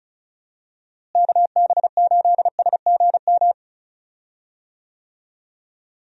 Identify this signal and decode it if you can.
Morse code, heard as a keyed tone, about 34 wpm, 700 Hz. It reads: KB8SGM
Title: USA Call Signs - 6 Characters - Repeated 3x - 35wpm